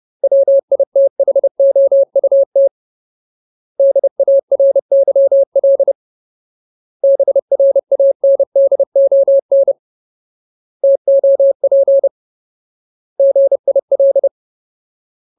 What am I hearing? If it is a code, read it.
Morse code, heard as a keyed tone, about 30 wpm, 550 Hz. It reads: WITHOUT DARYL BRANDON TOP GIL
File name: Ultimate Word List / 30WPM Ultimate